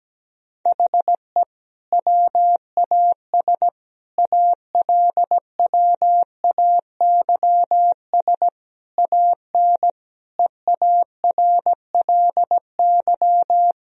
Morse code, heard as a keyed tone, about 17 wpm, 700 Hz. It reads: HE WAS ALWAYS AN EARLY
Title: Sentences from Top 1000 Words - Review Part 2 - 17wpm